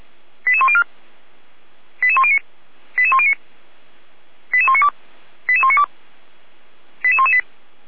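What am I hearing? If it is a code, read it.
Five-tone selective call (ZVEI), heard as quick runs of five tones, tones of about 70 ms. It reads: 88104 88109 88109 88192 88192 88118
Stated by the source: ZVEI five-tone selective call received by SDR, from the feuerwehr-orlamuende.org fire brigade audio server.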